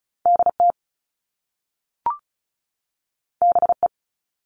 Morse code, heard as a keyed tone, about 36 wpm, 690 Hz. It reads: DT BE
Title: All Cut Numbers - 2-Digit 35wpm